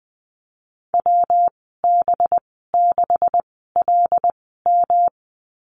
Morse code, heard as a keyed tone, about 20 wpm, spaced as for 15 wpm, 700 Hz. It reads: WB6LM
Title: 193 Lesson 31g - Letter v - Call Signs 20wpm